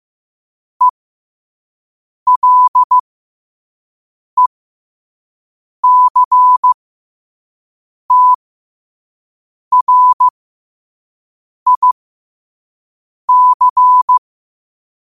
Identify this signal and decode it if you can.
Morse code, heard as a keyed tone, about 15 wpm, spaced as for 5 wpm, 1000 Hz. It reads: ELECTRIC
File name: qotdfiles / 5wpm